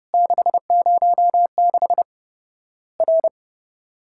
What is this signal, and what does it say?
Morse code, heard as a keyed tone, about 30 wpm, 700 Hz. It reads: 606 R
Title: Worked All Europe DX Contest - QTC Exchange - 30wpm